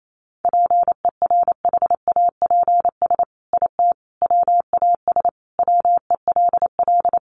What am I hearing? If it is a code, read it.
Morse code, heard as a keyed tone, about 28 wpm, 700 Hz. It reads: PERHAPS IT WAS WELL
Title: Sentences from Top 1400 Words 28wpm